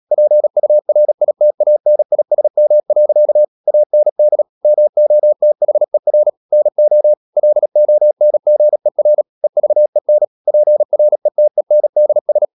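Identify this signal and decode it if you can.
Morse code, heard as a keyed tone, about 37 wpm, 600 Hz. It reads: PURITANISM. AND MOTHER NO LONGER EVEN PRETENDS
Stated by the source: Morse practice at 36 wpm, text of Ken Follett's novel Never, from CW med Gnister.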